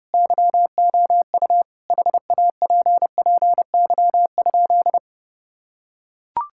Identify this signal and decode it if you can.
Morse code, heard as a keyed tone, about 32 wpm, 700 Hz. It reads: YOU HAPPY?
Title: Sentences from Top 1000 Words - Review Part 5 - 30wpm